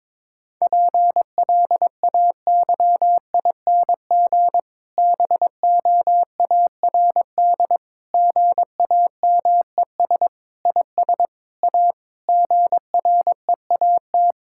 Morse code, heard as a keyed tone, about 22 wpm, 700 Hz. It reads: PLAYING BOARD GAMES IS A GREAT
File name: Difficult Sentences with Hints - 22wpm